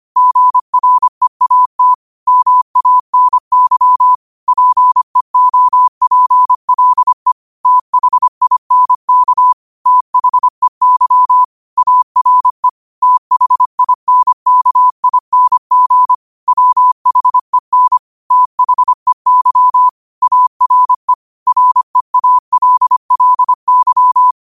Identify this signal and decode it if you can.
Morse code, heard as a keyed tone, about 25 wpm, 1000 Hz. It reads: GREAT MANY PEOPLE THINK THEY ARE THINKING WHEN THEY ARE REALLY
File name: qotdfiles / 25wpm